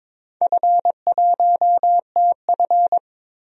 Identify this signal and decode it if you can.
Morse code, heard as a keyed tone, about 22 wpm, 700 Hz. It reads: F1TF
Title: DXCC-100 Call Signs - 4 Characters - 22wpm